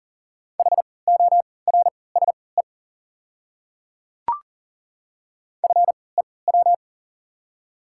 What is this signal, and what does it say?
Morse code, heard as a keyed tone, about 39 wpm, spaced as for 22 wpm, 690 Hz. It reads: HORSE FEW
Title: 3x Character Spacing - Top 500 Words - 40wpm